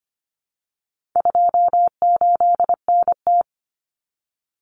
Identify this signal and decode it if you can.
Morse code, heard as a keyed tone, about 25 wpm, 700 Hz. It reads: 28NT
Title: Common Cut Numbers - 4-Digit 25wpm